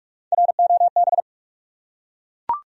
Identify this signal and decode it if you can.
Morse code, heard as a keyed tone, about 42 wpm, 700 Hz. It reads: ROB
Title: USA Common Names - Max 4 Letters - 45wpm